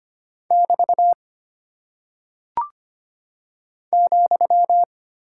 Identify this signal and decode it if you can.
Morse code, heard as a keyed tone, about 25 wpm, 700 Hz. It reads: <BT> ,
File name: Single Punctuation-Prosign - 25wpm